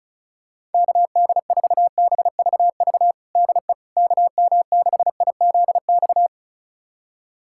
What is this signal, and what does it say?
Morse code, heard as a keyed tone, about 34 wpm, 700 Hz. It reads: KD4BVV DE KM6IZX
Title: USA Callsign DE Callsign - 6 Characters - 35wpm